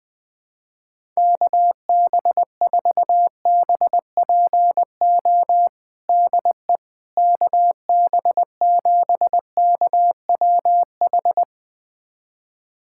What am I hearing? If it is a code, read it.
Morse code, heard as a keyed tone, about 21 wpm, 700 Hz. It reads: KB4BPO DE KB7KWH
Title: USA Callsign DE Callsign - 6 Characters - 20wpm